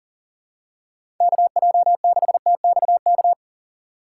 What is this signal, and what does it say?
Morse code, heard as a keyed tone, about 37 wpm, 700 Hz. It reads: KJ6TXK
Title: USA Call Signs - 6 Characters - 2x Character Spacing - 40wpm